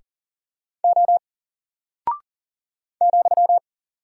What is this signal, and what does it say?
Morse code, heard as a keyed tone, about 38 wpm, 700 Hz. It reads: O ,
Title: Single Character (Letter:Number:Punctuation:Prosign) 40wpm